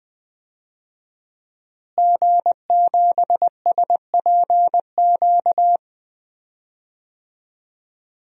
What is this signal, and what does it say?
Morse code, heard as a keyed tone, about 21 wpm, 700 Hz. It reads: G7SPQ
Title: DXCC-100 Call Signs - 5 Characters - Repeated 2x - 20wpm